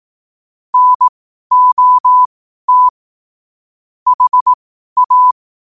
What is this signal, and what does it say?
Morse code, heard as a keyed tone, about 18 wpm, spaced as for 12 wpm, 1000 Hz. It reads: NOT HA
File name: qotdfiles / 12wpm